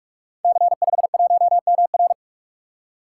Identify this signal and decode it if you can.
Morse code, heard as a keyed tone, about 44 wpm, 700 Hz. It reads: C51MR
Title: Code Groups - 5 Letter:Numbers - 45wpm